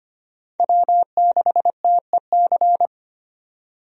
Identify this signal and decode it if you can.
Morse code, heard as a keyed tone, about 25 wpm, 700 Hz. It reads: W6TEC
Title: USA Call Signs - 5 Characters - 25wpm